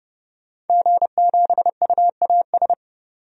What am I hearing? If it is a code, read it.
Morse code, heard as a keyed tone, about 30 wpm, 700 Hz. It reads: G7UAS
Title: DXCC-100 Call Signs - 5 Characters - 30wpm